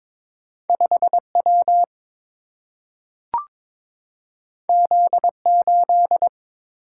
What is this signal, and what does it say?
Morse code, heard as a keyed tone, about 22 wpm, 700 Hz. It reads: 5W Z8
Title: Single-Letter Single-Number 22wpm